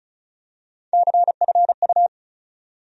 Morse code, heard as a keyed tone, about 34 wpm, 700 Hz. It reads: CFU
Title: USA Call Sign Postfixes - 3 Letters 35wpm